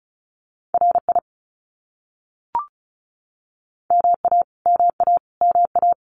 Morse code, heard as a keyed tone, about 34 wpm, 700 Hz. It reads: RI MA MA MA
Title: US States Repeated 3x 35wpm